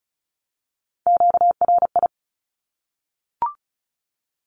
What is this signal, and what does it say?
Morse code, heard as a keyed tone, about 34 wpm, 700 Hz. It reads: QRI
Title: Q-Codes 35wpm